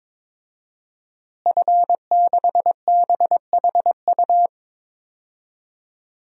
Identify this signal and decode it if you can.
Morse code, heard as a keyed tone, about 22 wpm, 700 Hz. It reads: F6BHU
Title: DXCC-100 Call Signs - 5 Characters - Repeated 2x - 22wpm